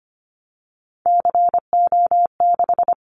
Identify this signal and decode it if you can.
Morse code, heard as a keyed tone, about 25 wpm, 700 Hz. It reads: CO6
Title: Callsign Prefixes for Top 100 DXCC - 25wpm